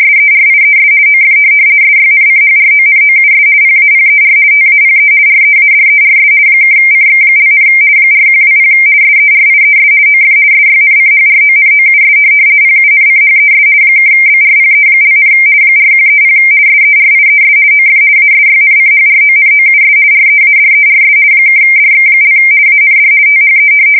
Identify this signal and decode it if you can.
RTTY, about 50 baud. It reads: SERIES OF SYNCHRONOUS PAPER TAPE READERS (CX), 4120 SERIES OF SYNCHRONOUS PUNCHES (BRPE), 4130 SERIES OF ASYNCHRONOUS READERS (DX) AND THE
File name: fsk170-50B / teletype-corp